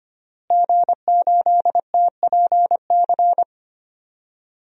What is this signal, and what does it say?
Morse code, heard as a keyed tone, about 25 wpm, 700 Hz. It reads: G8TPC
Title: DXCC-100 Call Signs - 5 Characters - Repeated 3x - 25wpm